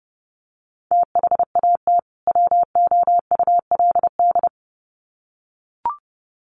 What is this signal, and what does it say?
Morse code, heard as a keyed tone, about 30 wpm, 700 Hz. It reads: THAT WOULD
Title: Sets of 2 Words 30wpm